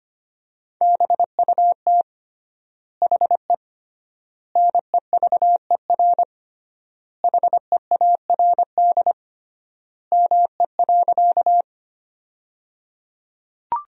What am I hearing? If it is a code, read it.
Morse code, heard as a keyed tone, about 25 wpm, 700 Hz. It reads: BUT HE NEVER HEARD ME.
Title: 3x Word-Spacing - Sentences from Top 400 Words - 25wpm